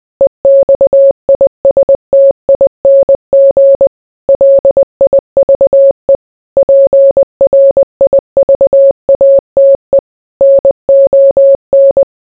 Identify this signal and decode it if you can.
Morse code, heard as a keyed tone, about 20 wpm, 550 Hz. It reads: EXISTING LIVE PRIVATE NON